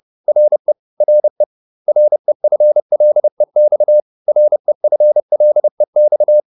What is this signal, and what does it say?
Morse code, heard as a keyed tone, about 30 wpm, 600 Hz. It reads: RE RE REFLEX REFLEX